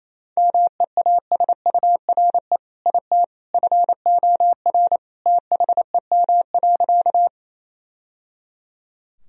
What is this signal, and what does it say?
Morse code, heard as a keyed tone, about 27 wpm, 700 Hz. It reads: MEASURE IT FOR THEM.
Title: Sentences from Top 1000 Words - Review Part 2 - 28wpm